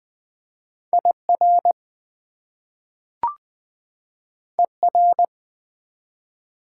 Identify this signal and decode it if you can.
Morse code, heard as a keyed tone, about 20 wpm, 700 Hz. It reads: IR ER
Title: Prefixes and Suffixes 20wpm